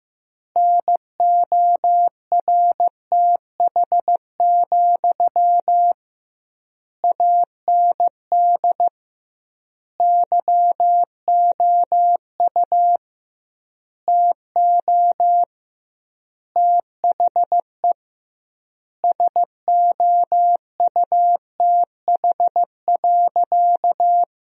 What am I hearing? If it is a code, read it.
Morse code, heard as a keyed tone, about 16 wpm, 700 Hz. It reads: NORTH, AND YOU TO THE SOUTH.
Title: 2x Word-Spacing - Sentences from Top 400 Words - 15wpm